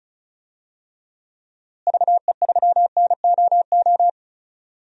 Morse code, heard as a keyed tone, about 34 wpm, 700 Hz. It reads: VE3NOO
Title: DXCC-100 Call Signs - 6 Characters - Repeated 2x - 35wpm